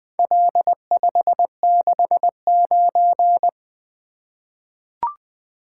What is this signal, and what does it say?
Morse code, heard as a keyed tone, about 20 wpm, 700 Hz. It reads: L569
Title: Four Letter-Number Combinations - 20wpm